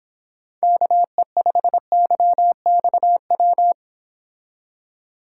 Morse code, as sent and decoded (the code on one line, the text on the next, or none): -.- . ..... -.-- -..- .--
KE5YXW